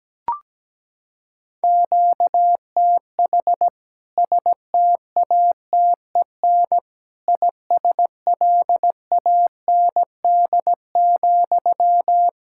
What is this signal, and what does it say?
Morse code, heard as a keyed tone, about 17 wpm, 700 Hz. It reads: QTH STATEN ISLAND,
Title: K1USN SST QSO Practice 17wpm